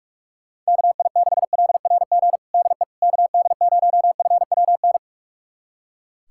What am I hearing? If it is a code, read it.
Morse code, heard as a keyed tone, about 41 wpm, 700 Hz. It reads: KI6LRG DE KD0FWN
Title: USA Callsign DE Callsign - 6 Characters - 45wpm